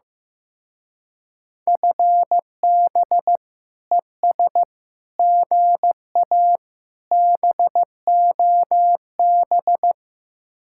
Morse code, heard as a keyed tone, about 15 wpm, 700 Hz. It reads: FB ES GA BOB
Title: Speed-Racing Common QSO Phrases Encore - 15wpm